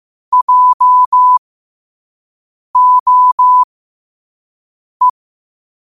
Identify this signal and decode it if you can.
Morse code, heard as a keyed tone, about 15 wpm, spaced as for 5 wpm, 1000 Hz. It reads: JOE
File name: qotdfiles / 5wpm